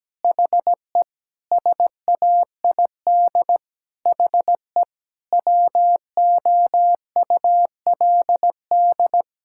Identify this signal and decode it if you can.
Morse code, heard as a keyed tone, about 17 wpm, 700 Hz. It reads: HE SAID HE WOULD
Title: Sentences from Top 1000 Words - Review Part 5 - 17wpm